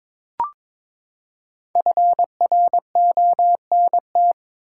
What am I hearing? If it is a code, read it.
Morse code, heard as a keyed tone, about 21 wpm, 700 Hz. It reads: FRONT
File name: Build A Phrase - 2-Word Phrases - 22wpm